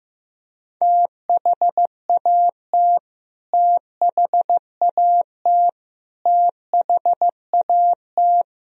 Morse code, heard as a keyed tone, about 15 wpm, 700 Hz. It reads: THAT THAT THAT
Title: Top 100 Words Repeated 15wpm